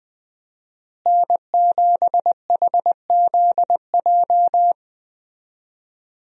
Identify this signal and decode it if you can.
Morse code, heard as a keyed tone, about 20 wpm, 700 Hz. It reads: N7HZJ
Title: USA Call Signs - 5 Characters - Repeated 3x - 20wpm